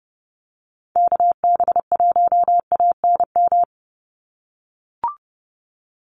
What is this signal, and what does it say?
Morse code, heard as a keyed tone, about 30 wpm, 700 Hz. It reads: KB1ANM
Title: USA Call Signs 30wpm